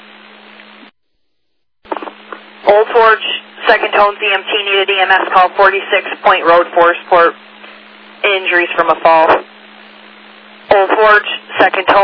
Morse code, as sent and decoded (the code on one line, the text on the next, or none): none